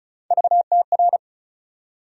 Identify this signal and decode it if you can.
Morse code, heard as a keyed tone, about 35 wpm, 700 Hz. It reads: VTR
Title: USA Call Sign Postfixes - 3 Letters 35wpm